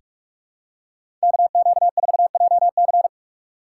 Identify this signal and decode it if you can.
Morse code, heard as a keyed tone, about 44 wpm, 700 Hz. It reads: KQ4JC
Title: USA Call Signs - 5 Characters - 6x Character Spacing - 45wpm